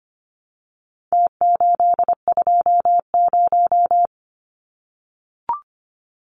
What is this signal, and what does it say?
Morse code, heard as a keyed tone, about 25 wpm, 700 Hz. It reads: T820
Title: Numbers - Contest Style - 4-Digit 25wpm